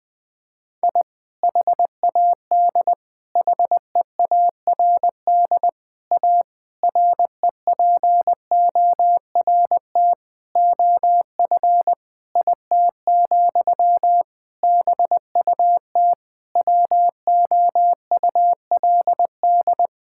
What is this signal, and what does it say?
Morse code, heard as a keyed tone, about 21 wpm, 700 Hz. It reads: I HAD HEARD A REPORT OF IT, BUT WOULD
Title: Sentences from Top 1000 Words - Review Part 6 - 20wpm